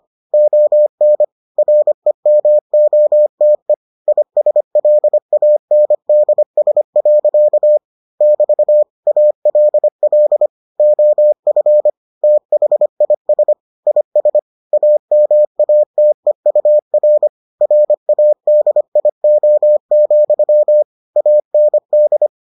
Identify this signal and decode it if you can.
Morse code, heard as a keyed tone, about 25 wpm, 600 Hz. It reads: ON REMOTE ISLANDS. <BT> ALL OF THIS IS AMATEUR RADIO, AND